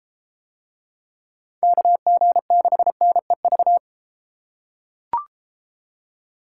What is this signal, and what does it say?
Morse code, heard as a keyed tone, about 33 wpm, 700 Hz. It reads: KG6NEV